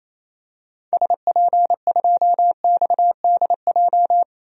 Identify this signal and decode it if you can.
Morse code, heard as a keyed tone, about 28 wpm, 700 Hz. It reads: SP2XDJ
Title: DXCC-100 Call Signs - 6 Characters - 28wpm